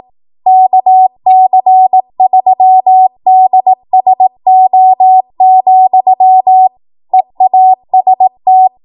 Morse code, heard as a keyed tone, about 18 wpm, 750 Hz. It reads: KC3DSO, EAST
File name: Morse / 18WPM